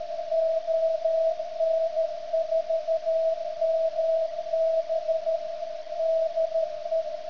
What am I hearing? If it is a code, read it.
Morse code, heard as a keyed tone, about 13 wpm, 650 Hz, with radio noise. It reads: ON4MB DE
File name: ft8-vs-cw / 90-to-150 CW 100hz